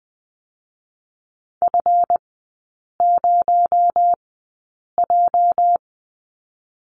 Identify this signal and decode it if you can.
Morse code, heard as a keyed tone, about 20 wpm, spaced as for 8 wpm, 700 Hz. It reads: F0J
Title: 218 Lesson 35e - Number 0 - Three Characters 20wpm